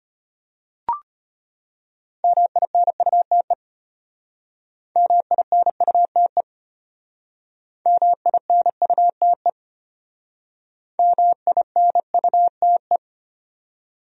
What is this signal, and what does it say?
Morse code, heard as a keyed tone, about 32 wpm, 700 Hz. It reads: MINUTE MINUTE MINUTE MINUTE